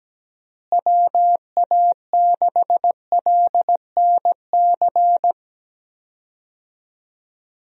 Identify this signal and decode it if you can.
Morse code, heard as a keyed tone, about 17 wpm, 700 Hz. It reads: WA6LNC
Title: USA Call Signs - 6 Characters - Repeated 3x - 17wpm